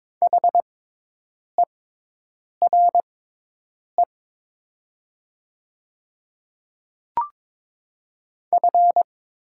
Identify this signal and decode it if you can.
Morse code, heard as a keyed tone, about 21 wpm, spaced as for 8 wpm, 700 Hz. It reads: HERE F
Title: 6x Character Spacing - Top 300 Words - 22wpm